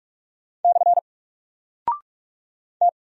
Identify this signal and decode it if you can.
Morse code, heard as a keyed tone, about 44 wpm, 700 Hz. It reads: / T
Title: Single Character (Letter:Number:Punctuation:Prosign) 45wpm